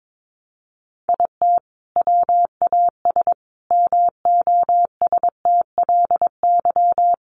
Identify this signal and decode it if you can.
Morse code, heard as a keyed tone, about 22 wpm, 700 Hz. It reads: IT WAS MOSTLY
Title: Sentences from Top 1300 Words 22wpm